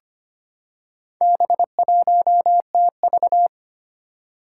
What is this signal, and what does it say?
Morse code, heard as a keyed tone, about 25 wpm, 700 Hz. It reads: B1TV
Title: Four Challenging Letter-Number Combinations 25wpm